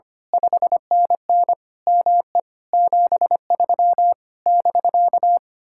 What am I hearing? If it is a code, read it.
Morse code, heard as a keyed tone, about 24 wpm, 700 Hz. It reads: 5NN ME 73 <BK>
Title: Speed-Racing Common QSO Phrases Encore - 25wpm